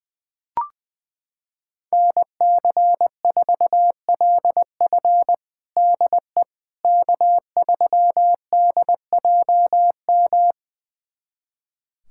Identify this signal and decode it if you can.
Morse code, heard as a keyed tone, about 21 wpm, 700 Hz. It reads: NC4LF DE K3DJM
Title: USA Callsign DE Callsign - 5 Characters - 20wpm